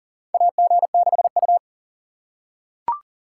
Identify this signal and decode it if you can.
Morse code, heard as a keyed tone, about 40 wpm, 700 Hz. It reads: AG6U
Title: USA Call Signs - 2x Character Spacing - 40wpm